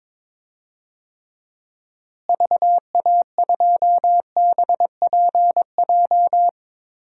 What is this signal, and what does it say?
Morse code, heard as a keyed tone, about 21 wpm, 700 Hz. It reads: VA2BPJ
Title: DXCC-100 Call Signs - 6 Characters - Repeated 3x - 22wpm